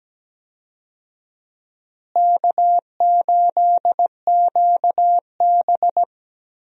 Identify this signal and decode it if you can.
Morse code, heard as a keyed tone, about 17 wpm, 700 Hz. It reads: K8QB
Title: USA Call Signs - 4 Characters - Repeated 3x - 17wpm